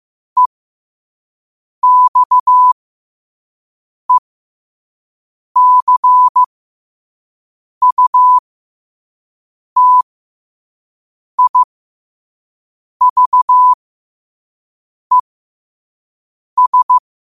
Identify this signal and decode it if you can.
Morse code, heard as a keyed tone, about 15 wpm, spaced as for 5 wpm, 1000 Hz. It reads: EXECUTIVES
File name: qotdfiles / 5wpm